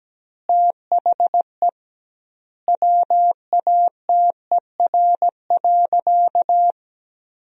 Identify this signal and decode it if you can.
Morse code, heard as a keyed tone, about 17 wpm, 700 Hz. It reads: THE WATER.
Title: 2x Word-Spacing - Sentences from Top 500 Words - 17wpm